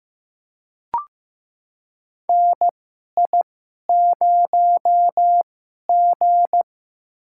Morse code, heard as a keyed tone, about 15 wpm, spaced as for 11 wpm, 700 Hz. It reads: NI0G
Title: USA Call Signs - 4 Characters - 2x Character Spacing - 15wpm